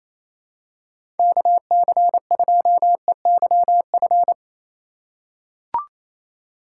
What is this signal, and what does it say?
Morse code, heard as a keyed tone, about 28 wpm, 700 Hz. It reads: KC2EYF
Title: USA Call Signs - 6 Characters - 3x Character Spacing - 28wpm